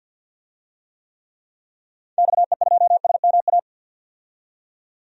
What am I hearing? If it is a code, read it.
Morse code, heard as a keyed tone, about 52 wpm, 700 Hz. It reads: XE2SMA
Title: DXCC-100 Call Signs - 6 Characters - 50wpm